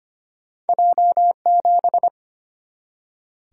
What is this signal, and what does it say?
Morse code, heard as a keyed tone, about 25 wpm, 700 Hz. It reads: J7
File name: Two Letter-Number Combinations - 25wpm